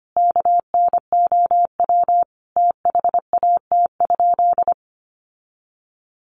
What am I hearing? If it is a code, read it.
Morse code, heard as a keyed tone, about 25 wpm, 700 Hz. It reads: KNOW THAT?
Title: Sentences from Top 1300 Words 25wpm